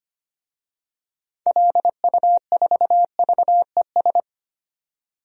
Morse code, heard as a keyed tone, about 25 wpm, 700 Hz. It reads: LU4VES
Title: DXCC-100 Call Signs - 6 Characters - 25wpm